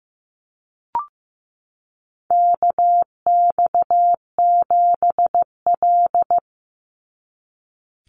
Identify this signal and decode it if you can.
Morse code, heard as a keyed tone, about 15 wpm, 700 Hz. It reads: KX7L
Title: NA SOTA Chaser Callsigns 15wpm